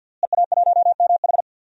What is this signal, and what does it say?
Morse code, heard as a keyed tone, about 47 wpm, 700 Hz. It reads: EA1MH
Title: DXCC-100 Call Signs - 5 Characters - Repeated 3x - 50wpm